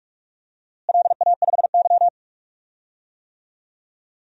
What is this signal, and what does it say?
Morse code, heard as a keyed tone, about 42 wpm, 700 Hz. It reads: LA5Y
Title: DXCC-100 Call Signs - 4 Characters - Repeated 2x - 45wpm